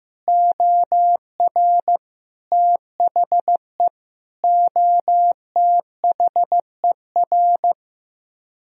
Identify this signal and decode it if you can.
Morse code, heard as a keyed tone, about 15 wpm, 700 Hz. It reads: OR THE OTHER
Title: Speed-Racing - Sets of 5 Words 15wpm